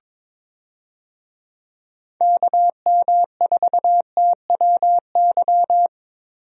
Morse code, heard as a keyed tone, about 22 wpm, 700 Hz. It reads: KM4TWY
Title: USA Call Signs - 6 Characters - Repeated 2x - 22wpm